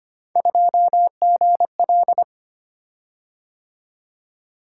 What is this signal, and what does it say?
Morse code, heard as a keyed tone, about 25 wpm, 700 Hz. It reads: 2GL
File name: Three Letter-Number Combinations - 25wpm